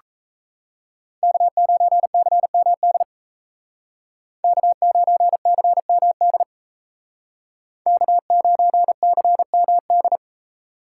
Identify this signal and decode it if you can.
Morse code, heard as a keyed tone, about 35 wpm, 700 Hz. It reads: K9CMD K9CMD K9CMD